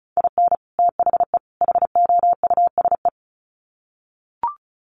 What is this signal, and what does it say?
Morse code, heard as a keyed tone, about 34 wpm, 700 Hz. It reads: IN THE HOUSE
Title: Sets of 3 Words 35wpm